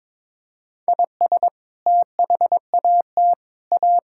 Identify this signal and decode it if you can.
Morse code, heard as a keyed tone, about 22 wpm, 700 Hz. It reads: IS THAT A
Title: Sentences from Top 1000 Words - Review Part 5 - 22wpm